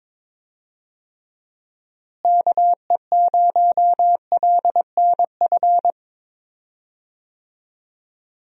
Morse code, heard as a keyed tone, about 22 wpm, 700 Hz. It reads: KE0LNF
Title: USA Call Signs - 6 Characters - Repeated 3x - 22wpm